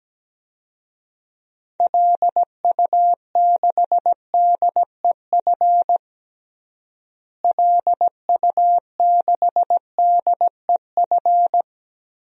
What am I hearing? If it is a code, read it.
Morse code, heard as a keyed tone, about 17 wpm, 700 Hz. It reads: LU6DEF LU6DEF
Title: DXCC-100 Call Signs - 6 Characters - Repeated 2x - 17wpm